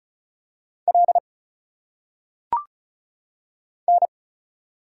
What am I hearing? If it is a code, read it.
Morse code, heard as a keyed tone, about 34 wpm, 700 Hz. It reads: L N
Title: Single Letters - Phonetic Pronunciation 35wpm